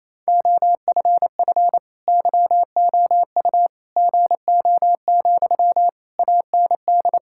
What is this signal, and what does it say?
Morse code, heard as a keyed tone, about 28 wpm, 700 Hz. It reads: OFF YOU GO, AND
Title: Sentences from Top 1000 Words - Review Part 6 - 28wpm